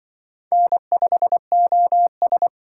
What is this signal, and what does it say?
Morse code, heard as a keyed tone, about 23 wpm, 700 Hz. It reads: N5OS
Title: Speed Racing - USA 1x2 Call Signs 20wpm